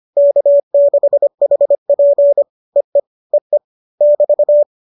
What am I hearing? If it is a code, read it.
Morse code, heard as a keyed tone, about 25 wpm, 570 Hz. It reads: K6HP EE EE <BT>
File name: qso205 25